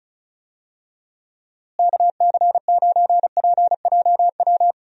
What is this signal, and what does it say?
Morse code, heard as a keyed tone, about 36 wpm, 700 Hz. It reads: KC9PJW
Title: USA Call Signs - 6 Characters - Repeated 3x - 35wpm